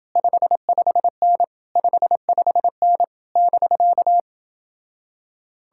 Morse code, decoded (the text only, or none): 55N 55N <BK>